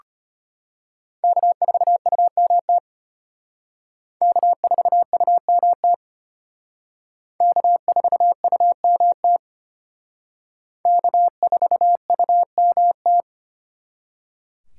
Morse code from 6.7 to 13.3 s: K4UMT K4UMT

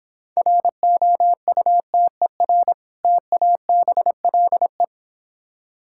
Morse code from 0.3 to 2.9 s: ROUTER

